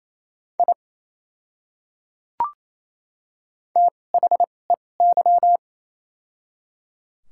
Morse code, heard as a keyed tone, about 27 wpm, spaced as for 20 wpm, 700 Hz. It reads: I THEY
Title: 2x Character Spacing - Top 200 Words - 28wpm